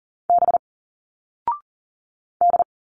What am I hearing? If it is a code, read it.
Morse code, heard as a keyed tone, about 40 wpm, 700 Hz. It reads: B D
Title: Single Letters 40wpm